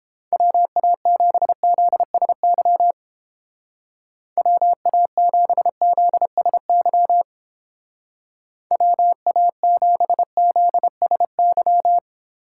Morse code, heard as a keyed tone, about 29 wpm, 700 Hz. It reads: WA7ZSY WA7ZSY WA7ZSY